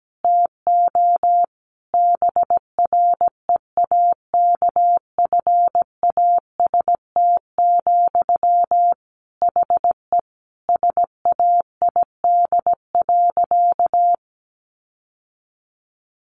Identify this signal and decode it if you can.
Morse code, heard as a keyed tone, about 17 wpm, 700 Hz. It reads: TO BREAKFAST, HE SAID.
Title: Sentences from Top 1500 Words 17wpm